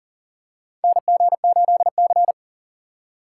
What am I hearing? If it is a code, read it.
Morse code, heard as a keyed tone, about 40 wpm, 700 Hz. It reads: NG8C
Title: USA Call Signs - 3x Character Spacing - 40wpm